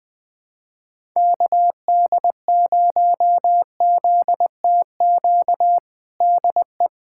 Morse code, heard as a keyed tone, about 20 wpm, 700 Hz. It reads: KD0ZTQ DE
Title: USA Callsign DE Callsign 20wpm